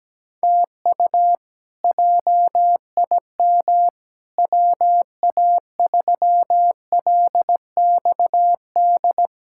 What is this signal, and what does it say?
Morse code, heard as a keyed tone, about 17 wpm, 700 Hz. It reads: TU JIM WA3LXD
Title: K1USN SST QSO Practice - 17wpm